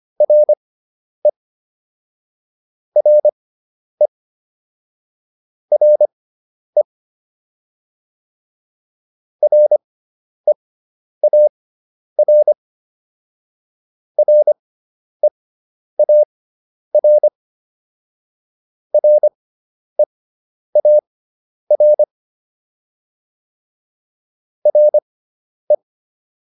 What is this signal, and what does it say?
Morse code, heard as a keyed tone, about 25 wpm, spaced as for 10 wpm, 600 Hz. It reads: RE RE RE REAR REAR REAR RE